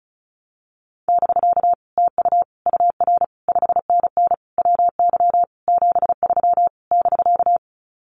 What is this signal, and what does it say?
Morse code, heard as a keyed tone, about 34 wpm, 700 Hz. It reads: <BK> TU UR 5NN WY 73 <BK>
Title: Common QSO Phrases Encore 35wpm